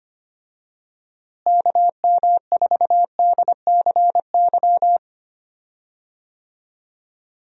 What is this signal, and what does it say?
Morse code, heard as a keyed tone, about 25 wpm, 700 Hz. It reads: KM4DCY
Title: USA Call Signs - 6 Characters - Repeated 2x - 25wpm